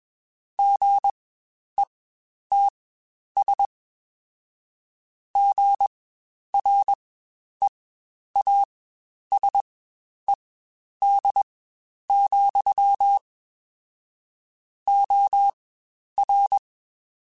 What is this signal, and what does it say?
Morse code, heard as a keyed tone, about 21 wpm, spaced as for 10 wpm, 800 Hz. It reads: GETS GREASED, OR
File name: Spaced / Chuckles Spaced-mp3s / Squeaky sp @20